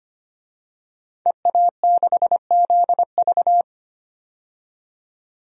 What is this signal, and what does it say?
Morse code, heard as a keyed tone, about 25 wpm, 700 Hz. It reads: EA6ZV
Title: DXCC-100 Call Signs - 5 Characters - 25wpm